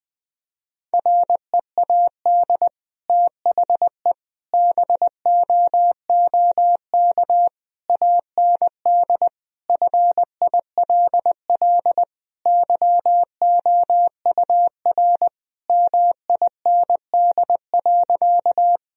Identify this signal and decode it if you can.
Morse code, heard as a keyed tone, about 20 wpm, 700 Hz. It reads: READ THE BOOK AND FILL YOUR MIND.
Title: Sentences from Top 1000 Words - Review Part 3 - 20wpm